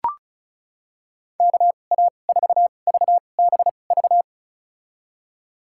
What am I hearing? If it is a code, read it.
Morse code, heard as a keyed tone, about 36 wpm, spaced as for 25 wpm, 700 Hz. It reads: KA4VBV
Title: USA Call Signs - 6 Characters - 2x Character Spacing - 35wpm